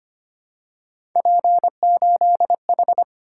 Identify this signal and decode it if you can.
Morse code, heard as a keyed tone, about 25 wpm, 700 Hz. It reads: P8H
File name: Three Letter-Number Combinations - 25wpm